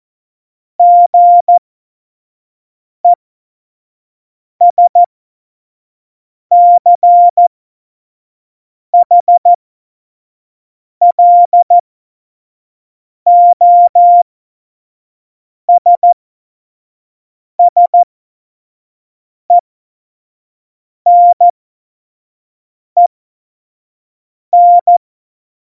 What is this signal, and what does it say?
Morse code, heard as a keyed tone, about 14 wpm, spaced as for 5 wpm, 700 Hz. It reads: GESCHLOSSENEN
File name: Morsekurs / Witze / Witz 19. Februar 2019, 25 F70